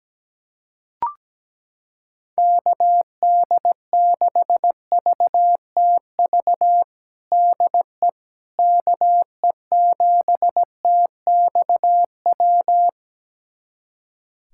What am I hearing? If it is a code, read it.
Morse code, heard as a keyed tone, about 17 wpm, 700 Hz. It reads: KD6VTV DE KE7TXW
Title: USA Callsign DE Callsign 17wpm